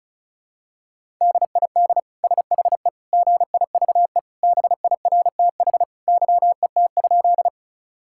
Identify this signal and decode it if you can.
Morse code, heard as a keyed tone, about 36 wpm, 700 Hz. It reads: DID SHE GIVE BIRTH YET?
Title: Sentences from Top 1000 Words - Review Part 5 - 35wpm